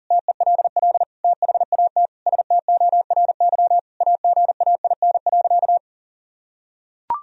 Morse code, heard as a keyed tone, about 40 wpm, 700 Hz. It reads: TELL THAT STORY AGAIN.
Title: Sentences from Top 1000 Words - Review Part 1 - 40wpm